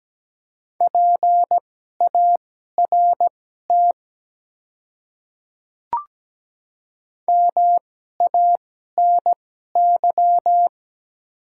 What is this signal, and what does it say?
Morse code, heard as a keyed tone, about 17 wpm, spaced as for 12 wpm, 700 Hz. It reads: PART MANY
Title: 2x Character Spacing - Top 200 Words - 17wpm